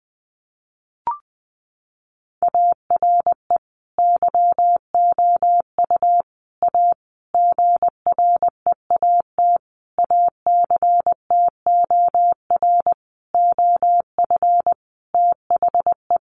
Morse code, heard as a keyed tone, about 20 wpm, 700 Hz. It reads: ARE YOU A GREAT ACTOR OF THE
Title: Sentences from Top 1500 Words 20wpm